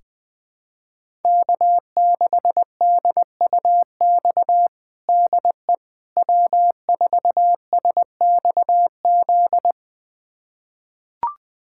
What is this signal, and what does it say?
Morse code, heard as a keyed tone, about 21 wpm, 700 Hz. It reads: K6DUX DE W4SXZ
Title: USA Callsign DE Callsign - 5 Characters - 20wpm